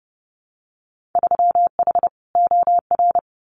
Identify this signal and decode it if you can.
Morse code, heard as a keyed tone, about 32 wpm, 700 Hz. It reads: 3H OR
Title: Winter Field Day Exchanges Encore 30wpm